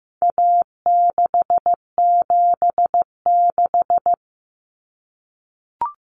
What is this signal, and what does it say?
Morse code, heard as a keyed tone, about 15 wpm, 700 Hz. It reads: A676
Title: Common Cut Numbers - 4-Digit 15wpm